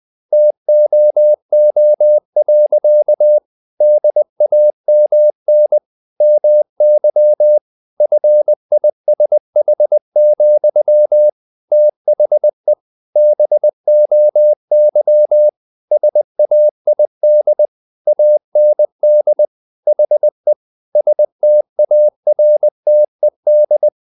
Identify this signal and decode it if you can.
Morse code, heard as a keyed tone, about 20 wpm, 600 Hz. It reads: TOO. DAMN MY FISH, THE BOY SAID AND HE STARTED
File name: the old man and the sea part 2 20wpm